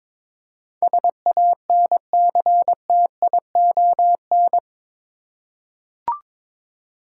Morse code, heard as a keyed tone, about 22 wpm, 700 Hz. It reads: SANCTION
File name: practice / 22wpm / Eight Letter Words - 22wpm